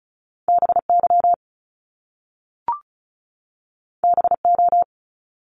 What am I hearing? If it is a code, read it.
Morse code, heard as a keyed tone, about 36 wpm, 700 Hz. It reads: BY BO
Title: Top Two Letter Combinations in Words 35wpm